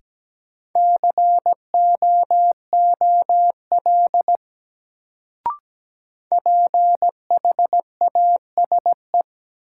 Morse code, heard as a keyed tone, about 17 wpm, 700 Hz. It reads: COOL PHASE
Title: Sentences from Top 1200 Words - New Word Drill - 17wpm